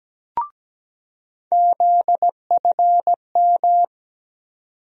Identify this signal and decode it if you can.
Morse code, heard as a keyed tone, about 17 wpm, 700 Hz. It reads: ZFM
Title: USA Call Sign Postfixes - 3 Letters 17wpm